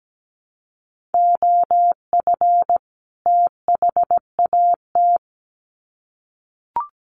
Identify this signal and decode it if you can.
Morse code, heard as a keyed tone, about 17 wpm, 700 Hz. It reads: OF THAT
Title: Sets of 2 Words 17wpm